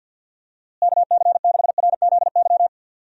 Morse code, heard as a keyed tone, about 52 wpm, 700 Hz. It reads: KC6RZY
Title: USA Call Signs - 6 Characters - Repeated 3x - 50wpm